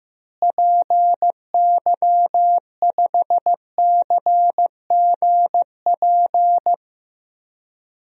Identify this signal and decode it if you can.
Morse code, heard as a keyed tone, about 15 wpm, 700 Hz. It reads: PY5CGP
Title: DXCC-100 Call Signs - 6 Characters - Repeated 3x - 15wpm